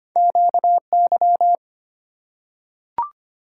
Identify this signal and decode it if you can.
Morse code, heard as a keyed tone, about 25 wpm, 700 Hz. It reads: QY
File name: Two Letter-Number Combinations - 25wpm